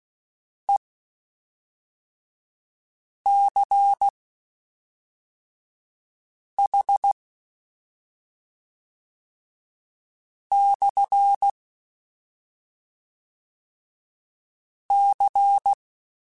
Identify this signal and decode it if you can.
Morse code, heard as a keyed tone, about 16 wpm, spaced as for 3 wpm, 800 Hz. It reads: ECH/C
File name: QSOs / QSOs-mp3s / WD0BGP 1@5